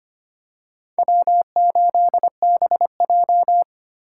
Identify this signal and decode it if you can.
Morse code, heard as a keyed tone, about 25 wpm, 700 Hz. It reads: W8BJ
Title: USA Call Signs - 4 Characters - 4x Character Spacing - 25wpm